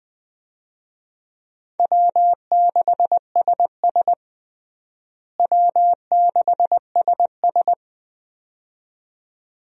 Morse code, heard as a keyed tone, about 20 wpm, 700 Hz. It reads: W6SS W6SS
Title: USA Call Signs - 4 Characters - Repeated 2x - 20wpm